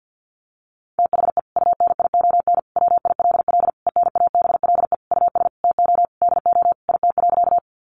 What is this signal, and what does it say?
Morse code, heard as a keyed tone, about 50 wpm, 700 Hz. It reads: THE UNION WILL ENABLE US TO DO IT.
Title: Sentences from Top 1500 Words 50wpm